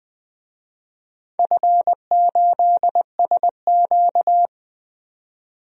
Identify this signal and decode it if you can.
Morse code, heard as a keyed tone, about 20 wpm, 700 Hz. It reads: F8SQ
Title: DXCC-100 Call Signs - 4 Characters - 20wpm